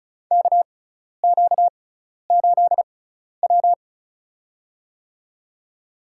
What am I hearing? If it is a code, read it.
Morse code, heard as a keyed tone, about 34 wpm, spaced as for 12 wpm, 700 Hz. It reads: KQ8W
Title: USA Call Signs - 4 Characters - 6x Character Spacing - 35wpm